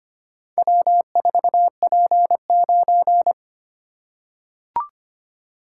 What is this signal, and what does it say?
Morse code, heard as a keyed tone, about 25 wpm, 700 Hz. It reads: W4P9
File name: practice / 25wpm / Four Letter-Number Combinations - 25wpm